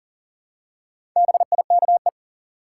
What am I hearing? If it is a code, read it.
Morse code, heard as a keyed tone, about 40 wpm, 700 Hz. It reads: BIKE
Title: Four Letter Words - 40wpm